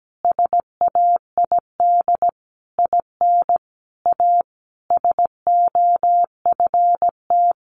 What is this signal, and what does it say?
Morse code, heard as a keyed tone, about 17 wpm, 700 Hz. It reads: SAID IN A SOFT
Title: Sentences from Top 1500 Words 17wpm